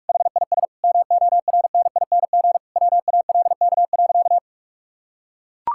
Morse code, heard as a keyed tone, about 43 wpm, 700 Hz. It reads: HIS MORNING WALK.
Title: Sentences from Top 1000 Words - Review Part 5 - 45wpm